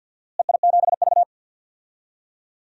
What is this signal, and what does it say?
Morse code, heard as a keyed tone, about 53 wpm, 700 Hz. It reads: EI7V
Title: DXCC-100 Call Signs - 4 Characters - Repeated 2x - 50wpm